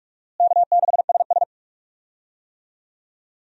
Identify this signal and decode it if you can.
Morse code, heard as a keyed tone, about 47 wpm, 700 Hz. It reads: K6SS
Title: USA Call Signs - 4 Characters - Repeated 2x - 45wpm